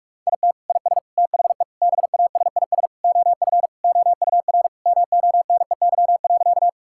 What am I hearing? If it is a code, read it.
Morse code, heard as a keyed tone, about 46 wpm, 700 Hz. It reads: IT IS THE BASIS OF OUR MONEY.
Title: Sentences from Top 1000 Words - Review Part 5 - 45wpm